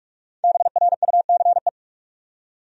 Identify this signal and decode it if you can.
Morse code, heard as a keyed tone, about 46 wpm, 700 Hz. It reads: BRUCE
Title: USA Common Names - Max 5 Letters - 45wpm